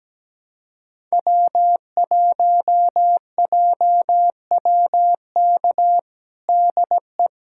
Morse code, heard as a keyed tone, about 17 wpm, 700 Hz. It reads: W1JWK DE
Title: USA Callsign DE Callsign - 5 Characters - 17wpm